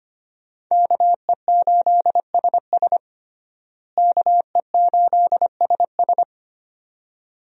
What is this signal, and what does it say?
Morse code, heard as a keyed tone, about 25 wpm, 700 Hz. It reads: KE8SS KE8SS
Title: USA Call Signs - 5 Characters - Repeated 2x - 25wpm